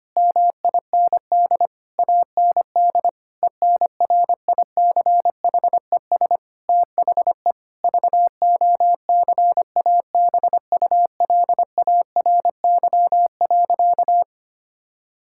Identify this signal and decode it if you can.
Morse code, heard as a keyed tone, about 25 wpm, 700 Hz. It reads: MIND AND ENRICHES THE VOCABULARY.
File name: practice / 25wpm / Difficult Sentences with Hints - 25wpm